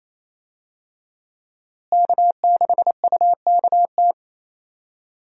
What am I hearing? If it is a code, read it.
Morse code, heard as a keyed tone, about 28 wpm, 700 Hz. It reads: K6UKT
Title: USA Call Signs - 5 Characters - Repeated 3x - 28wpm